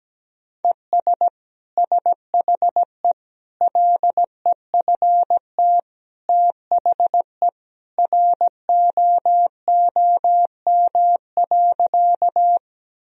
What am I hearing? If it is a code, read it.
Morse code, heard as a keyed tone, about 17 wpm, 700 Hz. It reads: ES SHE LEFT THE ROOM.
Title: Sentences from Top 300 Words and Abbreviations 17wpm